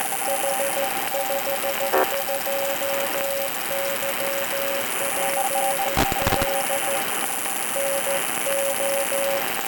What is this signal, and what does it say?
Morse code, heard as a keyed tone, about 14 wpm, 550 Hz, with noise. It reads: H52YPL NO